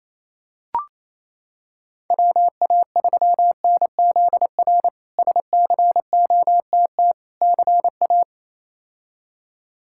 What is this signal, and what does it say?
Morse code, heard as a keyed tone, about 28 wpm, 700 Hz. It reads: WA3NZR SCOTT CA
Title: K1USN SST QSO Practice - 28wpm